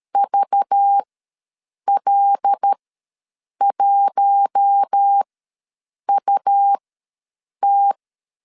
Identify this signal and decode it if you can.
Morse code, heard as a keyed tone, about 13 wpm, spaced as for 7 wpm, 800 Hz. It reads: VL1UT
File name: LEZIONI / 6 PKX0 / 6 PKX ws 12wpm 3106